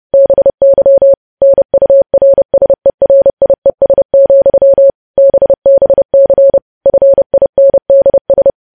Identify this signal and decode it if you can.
Morse code, heard as a keyed tone, about 30 wpm, 550 Hz. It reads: BY NURSERIES, BBC FINDS
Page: news headlines in morse code at 30 wpm